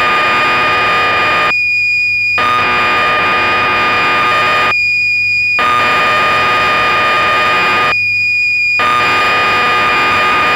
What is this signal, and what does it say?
Cassette data (Kansas City type).
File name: magazines / Acorn User / 1984-10 BBC Programs